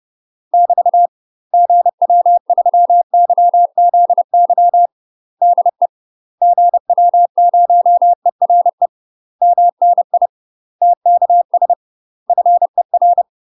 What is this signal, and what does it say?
Morse code, heard as a keyed tone, about 30 wpm, 700 Hz. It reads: <BT> GW3YZY DE GW0ERE MNI TKS FER
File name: UK MASTER QSOs TRAINING / 30WPM / UK MASTER QSOs 0005